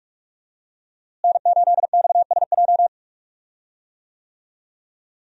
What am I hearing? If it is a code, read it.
Morse code, heard as a keyed tone, about 46 wpm, 700 Hz. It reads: N8XSJ
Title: USA Call Signs - 5 Characters - Repeated 3x - 45wpm